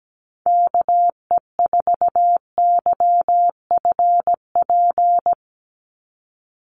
Morse code, heard as a keyed tone, about 17 wpm, 700 Hz. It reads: KE4YFP
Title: USA Call Signs 17wpm